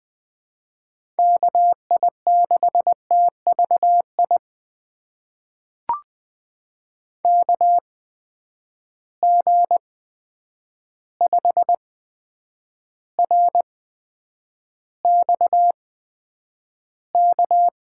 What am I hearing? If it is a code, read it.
Morse code, heard as a keyed tone, about 21 wpm, spaced as for 9 wpm, 700 Hz. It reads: KI6TVI KG5RXK
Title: USA Call Signs - 6 Characters - 8x Character Spacing - 20wpm